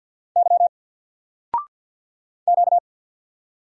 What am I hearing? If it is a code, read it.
Morse code, heard as a keyed tone, about 50 wpm, 700 Hz. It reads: Y Q
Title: Single Letter-Number Alt 50wpm